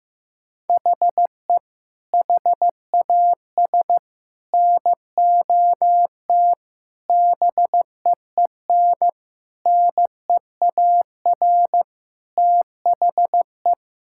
Morse code, heard as a keyed tone, about 15 wpm, 700 Hz. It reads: HE HAS NOT BEEN NEAR THE
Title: Sentences from Top 1000 Words - Review Part 6 - 15wpm